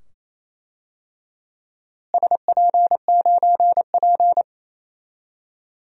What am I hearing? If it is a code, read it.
Morse code, heard as a keyed tone, about 28 wpm, 700 Hz. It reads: SP9P
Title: DXCC-100 Call Signs - 4 Characters - Repeated 2x - 28wpm